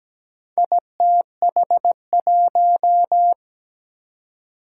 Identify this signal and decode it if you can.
Morse code, heard as a keyed tone, about 17 wpm, 700 Hz. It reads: ITH1
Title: Four Challenging Letter-Number Combinations 17wpm